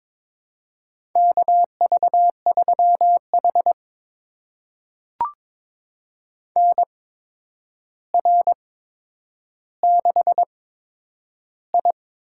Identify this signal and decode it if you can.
Morse code, heard as a keyed tone, about 22 wpm, spaced as for 9 wpm, 700 Hz. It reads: KV3H NR6I
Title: USA Call Signs - 4 Characters - 8x Character Spacing - 22wpm